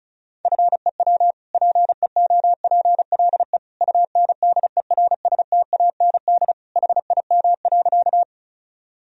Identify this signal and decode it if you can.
Morse code, heard as a keyed tone, about 34 wpm, 700 Hz. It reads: FEW PEOPLE UNDERSTAND HIM.
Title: Sentences from Top 1000 Words - Review Part 1 - 35wpm